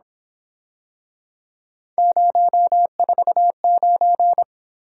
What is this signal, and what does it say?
Morse code, heard as a keyed tone, about 27 wpm, 700 Hz. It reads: 049
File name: Speed-Racing - Three Letter-Number Combinations - 22wpm